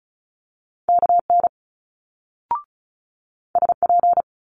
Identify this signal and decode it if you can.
Morse code, heard as a keyed tone, about 36 wpm, 700 Hz. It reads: KN SP
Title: Multi-letter Phonemes Quick Fire 35wpm